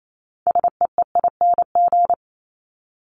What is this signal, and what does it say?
Morse code, heard as a keyed tone, about 28 wpm, 700 Hz. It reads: SEEING
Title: ING Suffix - Top 100 Words with Base Words 28wpm